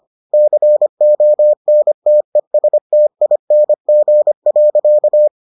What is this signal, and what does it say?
Morse code, heard as a keyed tone, about 25 wpm, 600 Hz. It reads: CONTESTING.